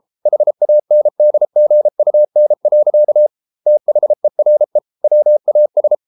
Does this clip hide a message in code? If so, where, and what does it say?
Yes: Morse code from 0.3 to 5.9 s, heard as a keyed tone, about 33 wpm, 600 Hz. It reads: HANDGUN. THERE WAS